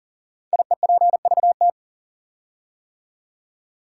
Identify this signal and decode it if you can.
Morse code, heard as a keyed tone, about 41 wpm, 700 Hz. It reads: IEPVT
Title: Code Groups - 5 Letters - 40wpm